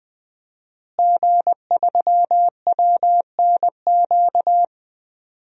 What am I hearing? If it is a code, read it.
Morse code, heard as a keyed tone, about 20 wpm, 700 Hz. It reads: G3WNQ
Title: DXCC-100 Call Signs - 5 Characters - Repeated 3x - 20wpm